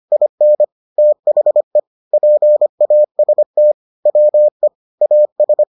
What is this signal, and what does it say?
Morse code, heard as a keyed tone, about 25 wpm, 600 Hz. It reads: IN THE PAST WE AS